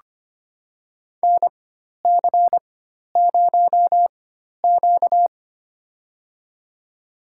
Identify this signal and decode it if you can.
Morse code, heard as a keyed tone, about 25 wpm, spaced as for 12 wpm, 700 Hz. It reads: NC0Q
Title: USA Call Signs - 4 Characters - 4x Character Spacing - 25wpm